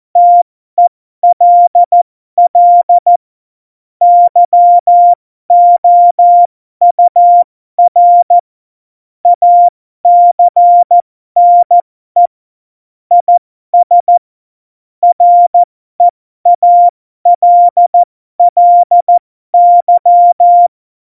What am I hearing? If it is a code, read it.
Morse code, heard as a keyed tone, about 14 wpm, spaced as for 12 wpm, 700 Hz. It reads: TELL YOUR ACNE IS REALLY
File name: Morsekurs / Witze